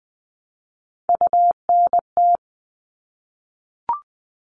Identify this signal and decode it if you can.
Morse code, heard as a keyed tone, about 20 wpm, 700 Hz. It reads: UNT
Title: All Cut Numbers - 3-Digit 20wpm